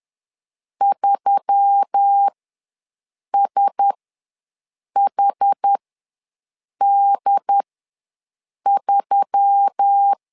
Morse code, heard as a keyed tone, about 10 wpm, spaced as for 6 wpm, 800 Hz. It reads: 3SHD3
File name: LEZIONI / 4 Q1S3 ws 10wpm 2437